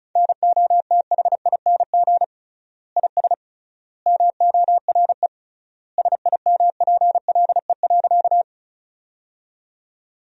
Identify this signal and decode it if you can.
Morse code, heard as a keyed tone, about 34 wpm, 700 Hz. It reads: NOTHING IS MORE SIMPLE.
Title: 3x Word-Spacing - Sentences from Top 500 Words - 35wpm